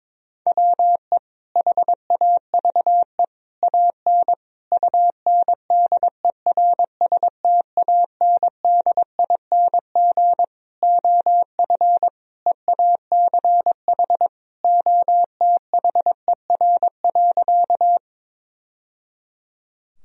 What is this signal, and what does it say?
Morse code, heard as a keyed tone, about 22 wpm, 700 Hz. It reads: WE HAVE AN UNDERSTANDING OF EACH OTHER.
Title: Sentences from Top 1000 Words - Review Part 3 - 22wpm